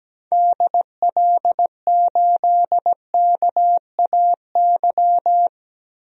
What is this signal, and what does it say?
Morse code, heard as a keyed tone, about 17 wpm, 700 Hz. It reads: DL8KAY
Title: DXCC-100 Call Signs - 6 Characters - 17wpm